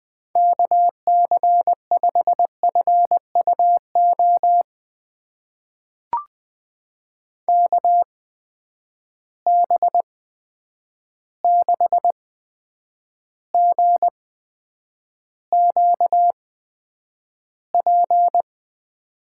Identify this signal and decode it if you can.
Morse code, heard as a keyed tone, about 20 wpm, spaced as for 9 wpm, 700 Hz. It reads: KC5FUO KB6GQP
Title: USA Call Signs - 6 Characters - 8x Character Spacing - 20wpm